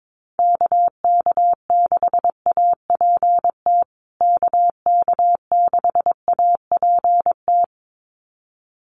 Morse code, heard as a keyed tone, about 22 wpm, 700 Hz. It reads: KK6APT KK6APT
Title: USA Call Signs Repeated 2x 22wpm